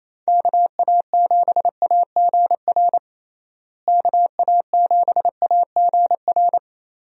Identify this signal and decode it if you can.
Morse code, heard as a keyed tone, about 28 wpm, 700 Hz. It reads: KA7AGR KA7AGR
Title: USA Call Signs - 6 Characters - Repeated 2x - 28wpm